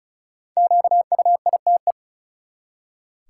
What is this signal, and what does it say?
Morse code, heard as a keyed tone, about 36 wpm, 700 Hz. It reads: QUITE
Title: Five Letter Words - 35wpm